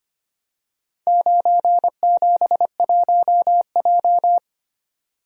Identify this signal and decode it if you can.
Morse code, heard as a keyed tone, about 25 wpm, 700 Hz. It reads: 971J
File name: practice / 25wpm / Four Letter-Number Combinations - 25wpm